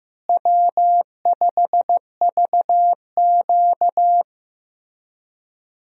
Morse code, heard as a keyed tone, about 15 wpm, 700 Hz. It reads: W5VQ
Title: USA Call Signs - 4 Characters - Repeated 3x - 15wpm